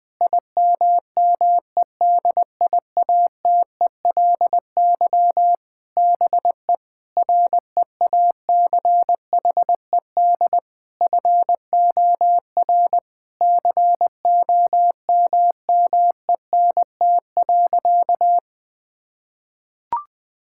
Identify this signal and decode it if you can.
Morse code, heard as a keyed tone, about 20 wpm, 700 Hz. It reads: IMMEDIATELY BE REACHED FOR COMMENT.
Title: Sentences with 7-Word Phrases - 20wpm